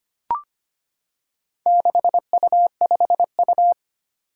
Morse code, heard as a keyed tone, about 25 wpm, 700 Hz. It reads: 6U5U
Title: Four Challenging Letter-Number Combinations 25wpm